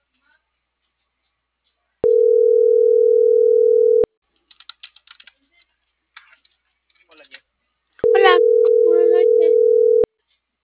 Ringback tone (North America), 440 + 480 Hz.